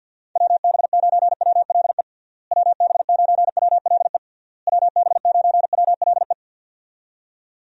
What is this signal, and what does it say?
Morse code, heard as a keyed tone, about 50 wpm, 700 Hz. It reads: WB9WLE WB9WLE WB9WLE
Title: USA Call Signs - 6 Characters - Repeated 3x - 50wpm